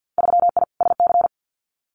Morse code, heard as a keyed tone, about 50 wpm, 700 Hz. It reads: 3I SC
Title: Winter Field Day Exchanges 50wpm